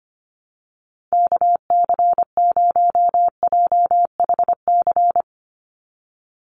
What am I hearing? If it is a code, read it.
Morse code, heard as a keyed tone, about 25 wpm, 700 Hz. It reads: KC0JHC
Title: USA Call Signs 25wpm